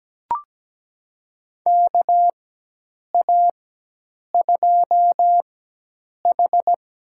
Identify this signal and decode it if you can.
Morse code, heard as a keyed tone, about 17 wpm, spaced as for 8 wpm, 700 Hz. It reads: KA2H
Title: USA Call Signs - 4x Character Spacing - 17wpm